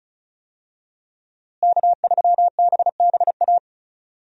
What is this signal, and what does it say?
Morse code, heard as a keyed tone, about 34 wpm, 700 Hz. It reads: K3BBA
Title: USA Call Signs - 5 Characters - Repeated 3x - 35wpm